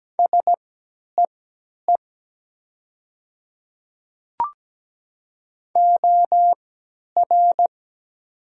Morse code, heard as a keyed tone, about 17 wpm, spaced as for 10 wpm, 700 Hz. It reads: SEE OR
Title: 3x Character Spacing - Top 100 Words - 17wpm